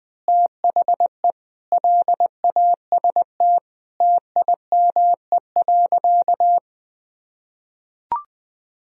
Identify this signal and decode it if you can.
Morse code, heard as a keyed tone, about 21 wpm, 700 Hz. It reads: THE LAST TIME.
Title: Sentences from Top 1000 Words - Review Part 2 - 20wpm